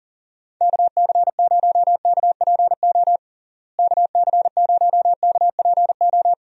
Morse code, heard as a keyed tone, about 40 wpm, 700 Hz. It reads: KC0KPO KC0KPO
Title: USA Call Signs - 6 Characters - Repeated 3x - 40wpm